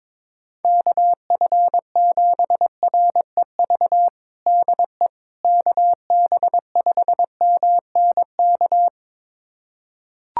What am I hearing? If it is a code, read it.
Morse code, heard as a keyed tone, about 23 wpm, 700 Hz. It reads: KF7REV DE KB5MNK
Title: USA Callsign DE Callsign - 6 Characters - 22wpm